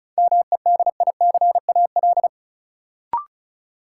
Morse code, heard as a keyed tone, about 34 wpm, 700 Hz. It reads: MEDICAL
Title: Sentences from Top 800 Words - New Word Drill - 35wpm